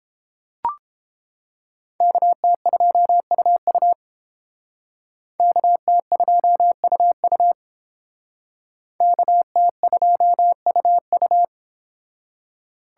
Morse code, heard as a keyed tone, about 28 wpm, 700 Hz. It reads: KT2UU KT2UU KT2UU